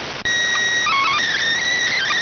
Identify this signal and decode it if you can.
SSTV (slow-scan television), PD 290.